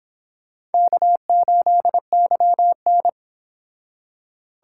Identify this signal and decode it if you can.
Morse code, heard as a keyed tone, about 26 wpm, 700 Hz. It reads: K8YN